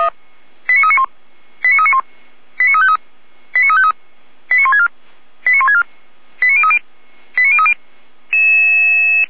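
Five-tone selective call (ZVEI), heard as quick runs of five tones, tones of about 70 ms. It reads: 89391 89391 89353 89353 89164 89164 89030 89030
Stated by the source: ZVEI five-tone selective call received by SDR, from feuerwehr-orlamuende.org (fire brigade Orlamünde) server.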